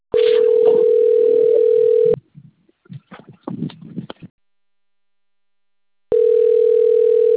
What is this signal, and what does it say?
Ringback tone (North America), 440 + 480 Hz.
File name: out- - - - - .14230